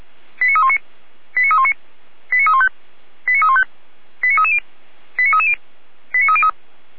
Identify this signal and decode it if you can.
Five-tone selective call (ZVEI), heard as quick runs of five tones, tones of about 70 ms. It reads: 89319 89319 89315 89315 89330 89330 89303
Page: SDR/ZVEI